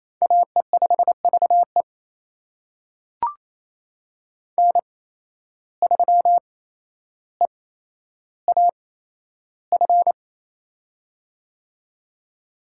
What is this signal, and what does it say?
Morse code, heard as a keyed tone, about 28 wpm, spaced as for 12 wpm, 700 Hz. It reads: AE5VE N3EAF
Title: USA Call Signs - 5 Characters - 8x Character Spacing - 28wpm